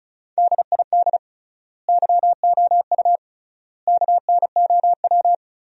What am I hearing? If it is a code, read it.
Morse code, heard as a keyed tone, about 36 wpm, 700 Hz. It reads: DID YOU KNOW
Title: 3x Word-Spacing - Sentences from Top 100 Words - 35wpm